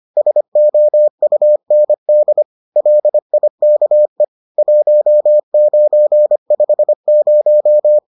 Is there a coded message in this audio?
Yes: Morse code from 0.2 to 8.0 s, heard as a keyed tone, about 25 wpm, 600 Hz. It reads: SOUND LIKE 1950